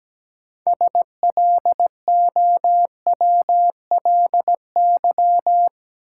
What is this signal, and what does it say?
Morse code, heard as a keyed tone, about 17 wpm, 700 Hz. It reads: SLOWLY
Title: Sentences from Top 1100 Words - New Word Drill - 17wpm